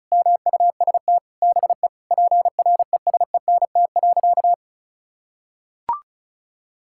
Morse code, heard as a keyed tone, about 36 wpm, 700 Hz. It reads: MUST BE PRESENT.
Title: Sentences from Top 1600 Words 35wpm